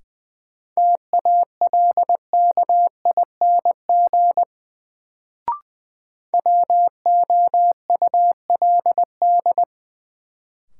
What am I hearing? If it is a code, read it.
Morse code, heard as a keyed tone, about 20 wpm, 700 Hz. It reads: TALKING WOULD
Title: Sets of 2 Words - New Word Drill - 20wpm